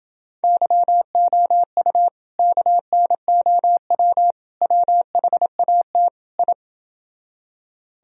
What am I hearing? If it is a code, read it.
Morse code, heard as a keyed tone, about 30 wpm, 700 Hz. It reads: YOU KNOW WHAT I